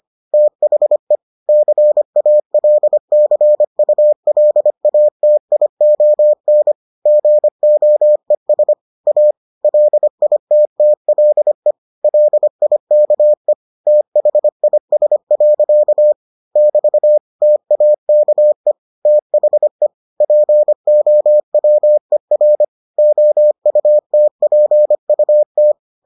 Morse code, heard as a keyed tone, about 25 wpm, 600 Hz. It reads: THE CALCULATION GOES A LITTLE LIKE THIS. <BT> TAKE THE POWER OUTPUT